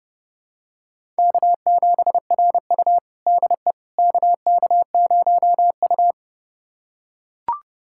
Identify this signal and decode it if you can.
Morse code, heard as a keyed tone, about 32 wpm, 700 Hz. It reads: K7RU DE KK0U
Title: USA Callsign DE Callsign - 4 Characters - 30wpm